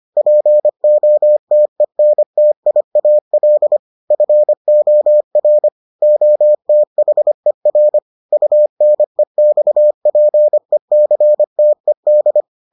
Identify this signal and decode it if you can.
Morse code, heard as a keyed tone, about 25 wpm, 600 Hz. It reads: POTENTIAL FOR OTHER UNEXPECTED